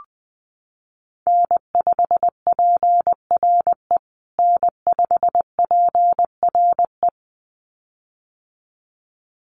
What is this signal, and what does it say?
Morse code, heard as a keyed tone, about 20 wpm, 700 Hz. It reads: N5PRE N5PRE
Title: USA Call Signs Repeated 2x 20wpm